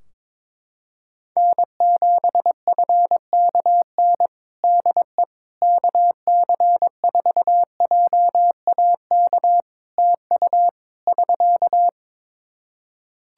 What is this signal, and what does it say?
Morse code, heard as a keyed tone, about 22 wpm, 700 Hz. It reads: N7FKN DE KC4JAK TU <SK>
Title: Practice QSO - Exchange 4 - Ending - 22wpm